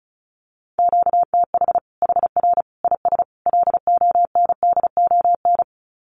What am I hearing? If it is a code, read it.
Morse code, heard as a keyed tone, about 34 wpm, 700 Hz. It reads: QTH HR IS LONDON
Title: Common QSO Phrases 35wpm